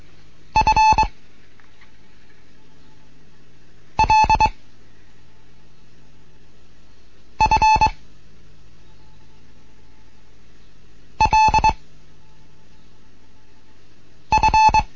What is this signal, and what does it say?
Morse code, heard as a keyed tone, about 22 wpm, 870 Hz. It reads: F L F L F